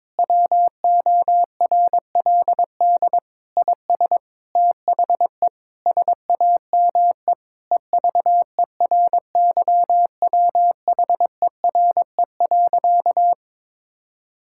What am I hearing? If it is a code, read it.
Morse code, heard as a keyed tone, about 22 wpm, 700 Hz. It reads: WORLD IS THE SAME EVERYWHERE.
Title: Sentences from Top 1600 Words 22wpm